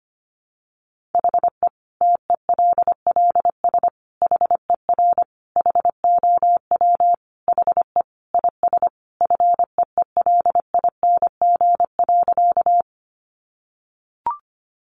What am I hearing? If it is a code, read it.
Morse code, heard as a keyed tone, about 25 wpm, 700 Hz. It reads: HE TELLS HER HOW HE IS FEELING.
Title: Additional Easy Phrases 25wpm